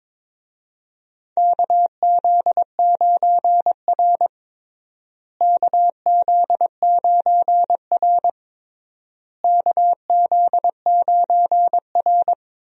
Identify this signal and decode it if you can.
Morse code, heard as a keyed tone, about 21 wpm, 700 Hz. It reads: KZ9R KZ9R KZ9R
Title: USA Call Signs - 4 Characters - Repeated 3x - 22wpm